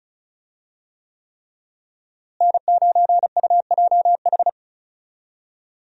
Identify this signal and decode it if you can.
Morse code, heard as a keyed tone, about 36 wpm, 700 Hz. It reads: N9UJH
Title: USA Call Signs - 5 Characters - Repeated 3x - 35wpm